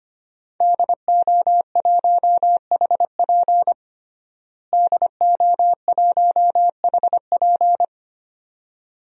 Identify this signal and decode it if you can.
Morse code, heard as a keyed tone, about 25 wpm, 700 Hz. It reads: DO1HP DO1HP
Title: DXCC-100 Call Signs - 5 Characters - Repeated 2x - 25wpm